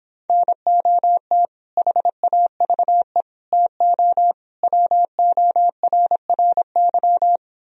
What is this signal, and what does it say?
Morse code, heard as a keyed tone, about 31 wpm, 700 Hz. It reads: NOT HAVE TO WORRY